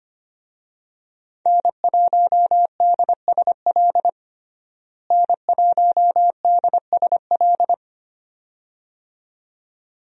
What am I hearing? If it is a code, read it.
Morse code, heard as a keyed tone, about 25 wpm, 700 Hz. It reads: N1DSL N1DSL
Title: USA Call Signs - 5 Characters - Repeated 2x - 25wpm